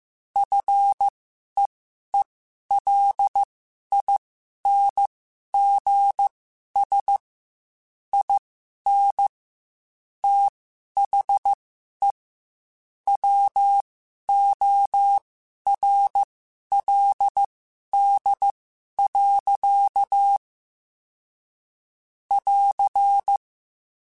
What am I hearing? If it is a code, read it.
Morse code, heard as a keyed tone, about 15 wpm, spaced as for 11 wpm, 800 Hz. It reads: FEELINGS IN THE WORLD. <AR>
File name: Quotes / Quotes-mp3s / ChildAsleep @13